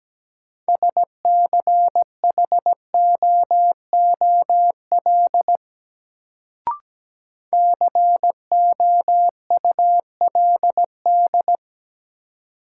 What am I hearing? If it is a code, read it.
Morse code, heard as a keyed tone, about 17 wpm, 700 Hz. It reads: SCHOOL COULD
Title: Sets of 2 Words - New Word Drill - 17wpm